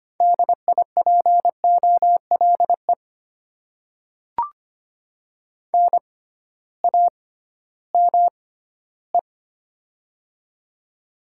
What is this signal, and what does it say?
Morse code, heard as a keyed tone, about 25 wpm, spaced as for 15 wpm, 700 Hz. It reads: DIPOLE NAME
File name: Top 90 QSO Elements - 6x Character Spacing - 25wpm